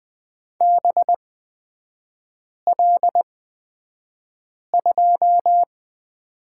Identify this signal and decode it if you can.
Morse code, heard as a keyed tone, about 20 wpm, spaced as for 5 wpm, 700 Hz. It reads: BL2
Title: 224 Lesson 36e - BT - Three Characters 20wpm